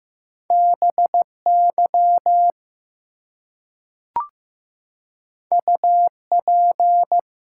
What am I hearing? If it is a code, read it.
Morse code, heard as a keyed tone, about 15 wpm, 700 Hz. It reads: BY UP
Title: Two Letter Words - 15wpm